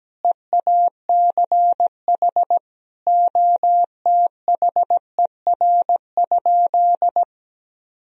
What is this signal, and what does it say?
Morse code, heard as a keyed tone, about 17 wpm, 700 Hz. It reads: EACH OTHER?
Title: Sentences from Top 1000 Words - Review Part 1 - 17wpm